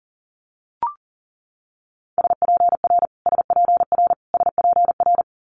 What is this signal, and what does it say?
Morse code, heard as a keyed tone, about 38 wpm, 700 Hz. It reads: SPR SPR SPR
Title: Multi-letter Phonemes Repeated 3x 40wpm